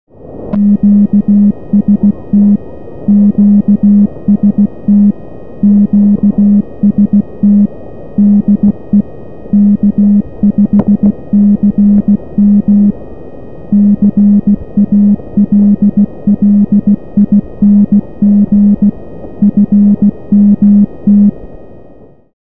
Morse code, heard as a keyed tone, about 16 wpm, 200 Hz. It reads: QST QST QST DE K5CM CALLING FMT
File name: K5CM 40m-byabi